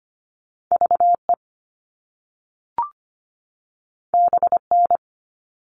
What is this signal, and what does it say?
Morse code, heard as a keyed tone, about 25 wpm, 700 Hz. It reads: VE BN
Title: All Cut Numbers - 2-Digit 25wpm